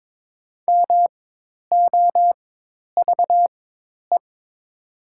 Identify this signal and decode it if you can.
Morse code, heard as a keyed tone, about 22 wpm, spaced as for 10 wpm, 700 Hz. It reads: MOVE
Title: 4x Character Spacing - Top 500 Words - 22wpm